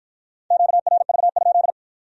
Morse code, heard as a keyed tone, about 51 wpm, 700 Hz. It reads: QRV?